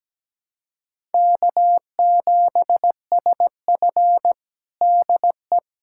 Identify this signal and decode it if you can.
Morse code, heard as a keyed tone, about 17 wpm, 700 Hz. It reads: K7SF DE
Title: K1USN SST QSO Practice 17wpm